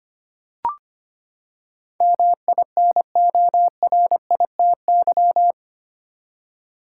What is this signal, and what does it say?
Morse code, heard as a keyed tone, about 25 wpm, 700 Hz. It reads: MINORITY
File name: Eight Letter Words - 25wpm